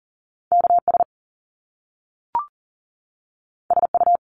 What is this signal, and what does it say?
Morse code, heard as a keyed tone, about 38 wpm, 700 Hz. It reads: KS SU
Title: Top Two Letter Combinations in Words 40wpm